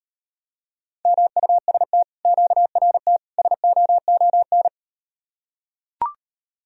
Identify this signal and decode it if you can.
Morse code, heard as a keyed tone, about 38 wpm, 700 Hz. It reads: MUST QRT SOON